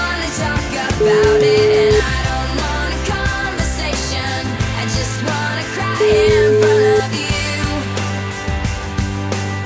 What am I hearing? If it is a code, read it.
Ringback tone (Europe, 425 Hz), with noise.